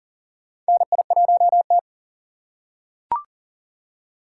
Morse code, heard as a keyed tone, about 40 wpm, 700 Hz. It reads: NI1T
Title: USA Call Signs - 4 Characters - 3x Character Spacing - 40wpm